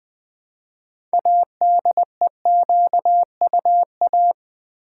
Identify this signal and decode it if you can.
Morse code, heard as a keyed tone, about 20 wpm, 700 Hz. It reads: ADEQUA
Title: Build A Word - 2 Letters At A Time - 8-Letter Words - 20wpm